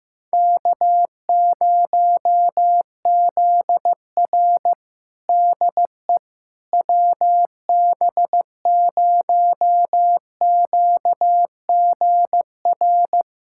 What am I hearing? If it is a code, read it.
Morse code, heard as a keyed tone, about 15 wpm, 700 Hz. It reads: K0ZR DE WB0QGR
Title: Practice QSO - Exchange 4 - Ending - 15wpm